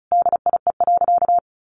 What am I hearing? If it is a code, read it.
Morse code, heard as a keyed tone, about 34 wpm, 700 Hz. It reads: DIE.
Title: Sentences from Top 1300 Words 35wpm